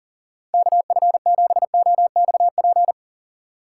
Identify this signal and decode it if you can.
Morse code, heard as a keyed tone, about 40 wpm, 700 Hz. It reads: KF7OXP
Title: USA Call Signs - 6 Characters - 4x Character Spacing - 40wpm